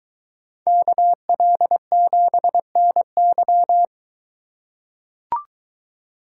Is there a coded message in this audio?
Yes: Morse code from 0.7 to 3.9 s, heard as a keyed tone, about 22 wpm, 700 Hz. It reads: KL7NY